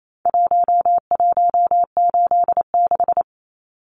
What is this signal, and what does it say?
Morse code, heard as a keyed tone, about 28 wpm, 700 Hz. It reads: 1186
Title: Numbers - Contest Style - 4-Digit 28wpm